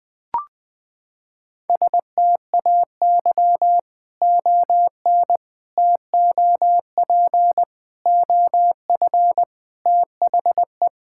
Morse code, heard as a keyed tone, about 20 wpm, 700 Hz. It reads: STAY ON TOP OF THE
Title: Sets of 6 Words - 20wpm